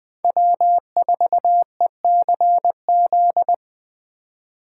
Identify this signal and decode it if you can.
Morse code, heard as a keyed tone, about 20 wpm, 700 Hz. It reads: W4ECZ
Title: Speed Racing - USA 1x3 Call Signs 17wpm